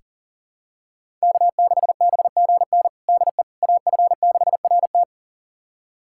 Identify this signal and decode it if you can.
Morse code, heard as a keyed tone, about 40 wpm, 700 Hz. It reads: K6BGN DE AF6RT
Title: USA Callsign DE Callsign - 5 Characters - 40wpm